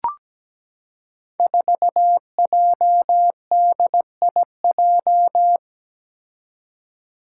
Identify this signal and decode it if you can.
Morse code, heard as a keyed tone, about 17 wpm, 700 Hz. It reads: 4JDIJ
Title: Five Challenging Letter-Number Combinations 17wpm